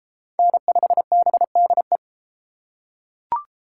Morse code, heard as a keyed tone, about 30 wpm, 700 Hz. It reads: N5BDE